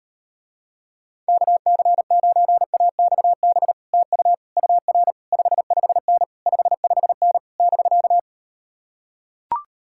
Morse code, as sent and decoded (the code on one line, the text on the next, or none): -.- -.-. ----. .- -..- -... / - ..- / ..- .-. / ..... ..... -. / ..... ..... -. / -...-.-
KC9AXB TU UR 55N 55N <BK>